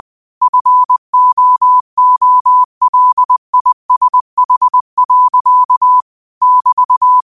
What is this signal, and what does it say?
Morse code, heard as a keyed tone, about 20 wpm, 1000 Hz. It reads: FOOLISH. <BT>
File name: qotdfiles / 20wpm